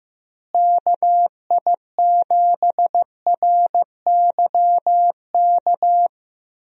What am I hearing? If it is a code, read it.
Morse code, heard as a keyed tone, about 15 wpm, 700 Hz. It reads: KI7RYK
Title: USA Call Signs - 6 Characters - Repeated 2x - 15wpm